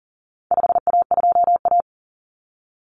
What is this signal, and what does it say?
Morse code, heard as a keyed tone, about 40 wpm, 700 Hz. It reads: 5A2A
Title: Common Cut Numbers - 4-Digit 40wpm